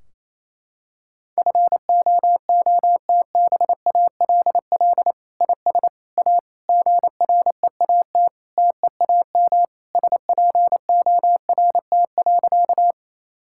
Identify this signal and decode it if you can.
Morse code, heard as a keyed tone, about 28 wpm, 700 Hz. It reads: FOOTBALL IS A GREAT TEAM SPORT.
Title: Sentences from Top 1600 Words 28wpm